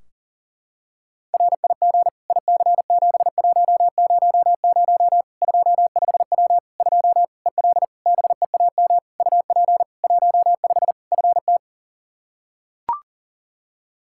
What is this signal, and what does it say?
Morse code, heard as a keyed tone, about 40 wpm, 700 Hz. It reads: RIG IC7100 25W 2 EL BEAM UP 15 FT
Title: K1USN SST QSO Practice 40wpm